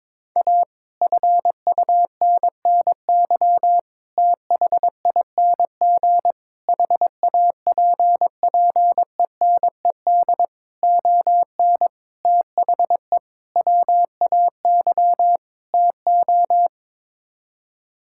Morse code, heard as a keyed tone, about 22 wpm, 700 Hz. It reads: A FUNNY THING HAPPENED ON THE WAY TO
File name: practice / 22wpm / Sets of 8 Words - 22wpm